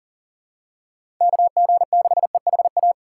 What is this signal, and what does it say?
Morse code, heard as a keyed tone, about 38 wpm, 700 Hz. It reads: KG6EHA
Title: USA Call Signs - 6 Characters - 4x Character Spacing - 40wpm